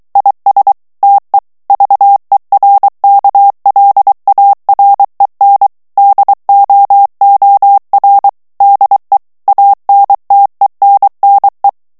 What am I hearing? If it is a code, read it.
Morse code, heard as a keyed tone, about 23 wpm, 800 Hz. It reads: IS TE VERKLAREN DOOR DE ANTENNE
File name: HOLLANDAIS texte clair 1 vitesse 22 mots minute